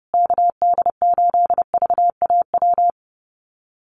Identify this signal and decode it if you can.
Morse code, heard as a keyed tone, about 30 wpm, 700 Hz. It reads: KD8VAW
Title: USA Call Signs Repeated 3x 30wpm